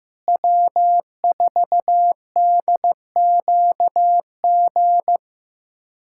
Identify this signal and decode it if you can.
Morse code, heard as a keyed tone, about 15 wpm, 700 Hz. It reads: W4DQG
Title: USA Callsign DE Callsign - 5 Characters - 15wpm